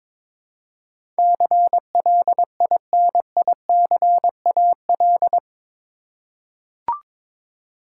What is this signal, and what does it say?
Morse code, heard as a keyed tone, about 22 wpm, 700 Hz. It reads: CLINICAL
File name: practice / 22wpm / Eight Letter Words - 22wpm